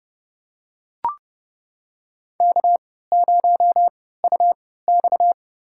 Morse code, heard as a keyed tone, about 30 wpm, spaced as for 17 wpm, 700 Hz. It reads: K0UX
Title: USA Call Signs - 3x Character Spacing - 30wpm